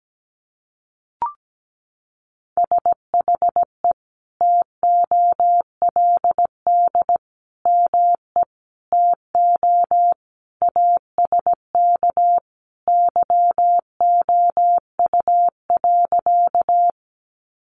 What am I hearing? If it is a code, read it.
Morse code, heard as a keyed tone, about 17 wpm, 700 Hz. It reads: SHE TOLD ME TO ASK YOU.
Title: Sentences from Top 400 Words 17wpm